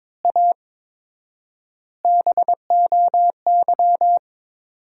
Morse code, heard as a keyed tone, about 22 wpm, 700 Hz. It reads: A BOY
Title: 4x Word-Spacing - Sentences from Top 400 Words - 22wpm